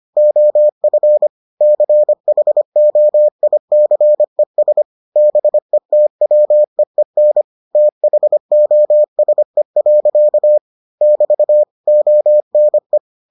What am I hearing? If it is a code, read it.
Morse code, heard as a keyed tone, about 25 wpm, 600 Hz. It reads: OF CHOICES BETWEEN THOSE. <BT> ONE